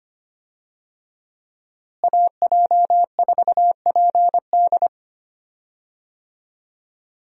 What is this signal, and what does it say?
Morse code, heard as a keyed tone, about 25 wpm, 700 Hz. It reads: AJ4PD
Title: USA Call Signs - 5 Characters - Repeated 2x - 25wpm